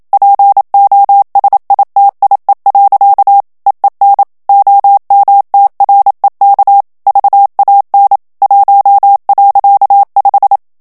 Morse code, heard as a keyed tone, about 27 wpm, 800 Hz. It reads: POSITIE. EEN OMTREK VAN 1.5
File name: HOLLANDAIS texte clair 1 vitesse 26 mots minute